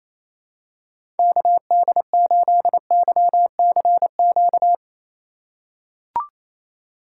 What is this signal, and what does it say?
Morse code, heard as a keyed tone, about 28 wpm, 700 Hz. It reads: KD8YCQ
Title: USA Call Signs - 6 Characters - 3x Character Spacing - 28wpm